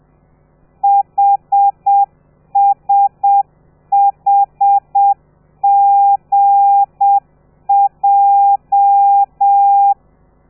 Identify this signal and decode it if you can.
Morse code, heard as a keyed tone, about 7 wpm, 800 Hz. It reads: HSHGJ
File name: telegrafia / i2tfj / Lez13 30